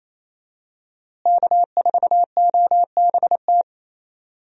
Morse code, heard as a keyed tone, about 28 wpm, 700 Hz. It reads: K4OBT
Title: USA Call Signs - 5 Characters - Repeated 3x - 28wpm